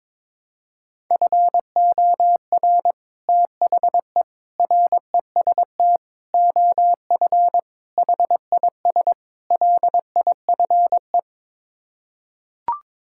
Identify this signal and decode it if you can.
Morse code, heard as a keyed tone, about 22 wpm, 700 Hz. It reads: FOR THE REST OF HIS LIFE
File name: practice / 22wpm / Sets of 6 Words - 22wpm